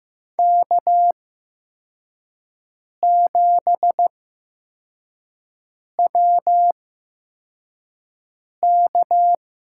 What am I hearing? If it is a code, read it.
Morse code, heard as a keyed tone, about 15 wpm, spaced as for 4 wpm, 700 Hz. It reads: K7WK
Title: USA Call Signs - 5 Characters - 8x Character Spacing - 15wpm